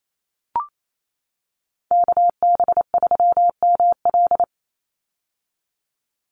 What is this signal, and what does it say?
Morse code, heard as a keyed tone, about 28 wpm, 700 Hz. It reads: KB3ML
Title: Call Signs - 28wpm